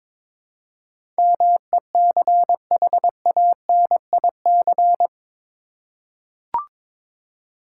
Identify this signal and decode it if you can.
Morse code, heard as a keyed tone, about 22 wpm, 700 Hz. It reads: MECHANIC
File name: practice / 22wpm / Eight Letter Words - 22wpm